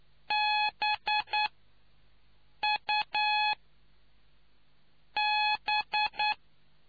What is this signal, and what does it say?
Morse code, heard as a keyed tone, about 9 wpm, spaced as for 5 wpm, 840 Hz. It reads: BUB